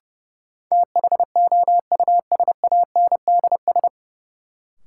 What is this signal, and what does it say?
Morse code, heard as a keyed tone, about 32 wpm, 700 Hz. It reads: THOUSANDS
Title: Sets of 4 Words - New Word Drill - 30wpm